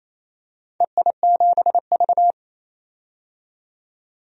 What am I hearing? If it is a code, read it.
Morse code, heard as a keyed tone, about 28 wpm, 700 Hz. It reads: EI7V
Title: DXCC-100 Call Signs - 4 Characters - 28wpm